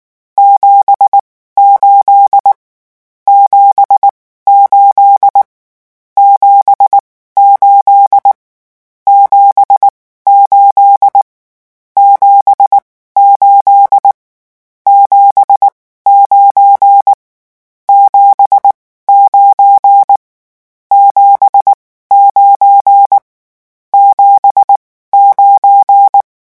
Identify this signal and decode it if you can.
Morse code, heard as a keyed tone, about 19 wpm, spaced as for 14 wpm, 800 Hz. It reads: 78 78 78 78 78 7 9 79 79 79
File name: morse mp3 / lecon12-vitesse 18